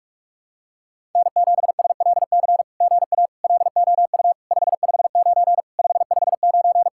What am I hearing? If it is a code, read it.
Morse code, heard as a keyed tone, about 46 wpm, 700 Hz. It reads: N7SLC GA LOU 559 559
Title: SOTA QSO Practice 45wpm